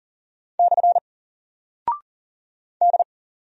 Morse code, heard as a keyed tone, about 40 wpm, 700 Hz. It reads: / D
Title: Single Character (Letter:Number:Punctuation:Prosign) 40wpm